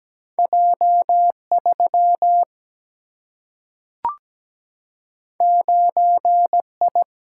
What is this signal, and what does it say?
Morse code, heard as a keyed tone, about 17 wpm, 700 Hz. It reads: J 3 9 I
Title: Single-Letter Single-Number 17wpm